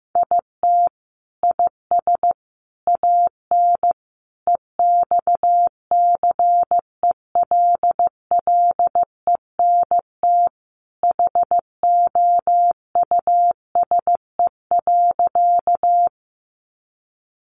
Morse code, heard as a keyed tone, about 15 wpm, 700 Hz. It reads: IT IS AN EXCELLENT HOUSE.
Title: Sentences from Top 1000 Words 15wpm